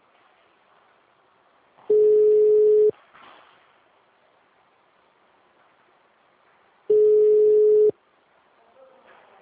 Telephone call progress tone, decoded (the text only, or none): ringback tone